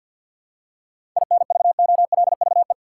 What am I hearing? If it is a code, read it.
Morse code, heard as a keyed tone, about 52 wpm, 700 Hz. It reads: INVOLVE
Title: Seven Letter Words - 50wpm